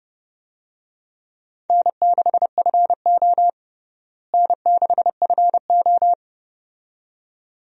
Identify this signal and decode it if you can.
Morse code, heard as a keyed tone, about 30 wpm, 700 Hz. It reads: N6FO N6FO
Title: USA Call Signs - 4 Characters - Repeated 2x - 30wpm